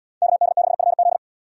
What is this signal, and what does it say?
Morse code, heard as a keyed tone, about 69 wpm, 700 Hz. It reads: KD6FZ